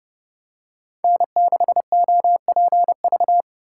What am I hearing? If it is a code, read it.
Morse code, heard as a keyed tone, about 31 wpm, 700 Hz. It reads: N6OPV